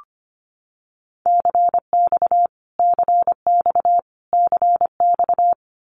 Morse code, heard as a keyed tone, about 25 wpm, 700 Hz. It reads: CX CX CX
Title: Abbreviations Repeated 3x - 25wpm